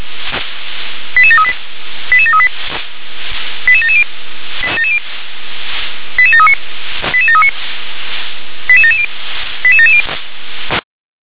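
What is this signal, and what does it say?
Five-tone selective call (ZVEI), heard as quick runs of five tones, tones of about 70 ms. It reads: 88738 88738 88770 88739 88739 88770 88770
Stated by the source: Five-tone ZVEI selective call received on SDR.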